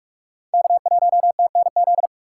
Morse code, heard as a keyed tone, about 46 wpm, 700 Hz. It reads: K1TNZ
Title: USA Call Signs - 6x Character Spacing - 45wpm